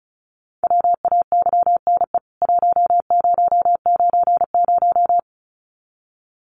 Morse code, heard as a keyed tone, about 34 wpm, 700 Hz. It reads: WAYNE 1090
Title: CWT Exchange 35wpm